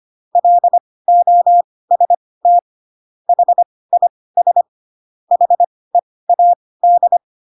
Morse code, heard as a keyed tone, about 25 wpm, spaced as for 18 wpm, 700 Hz. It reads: LOST HIS HEAD